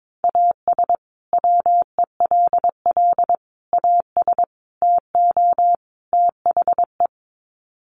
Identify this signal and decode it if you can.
Morse code, heard as a keyed tone, about 22 wpm, 700 Hz. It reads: AS WELL AS TO THE
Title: Sets of 5 Words Encore2 22wpm